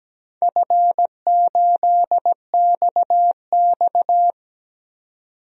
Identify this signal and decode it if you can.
Morse code, heard as a keyed tone, about 17 wpm, 700 Hz. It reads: F8XX
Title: DXCC-100 Call Signs - 4 Characters - 17wpm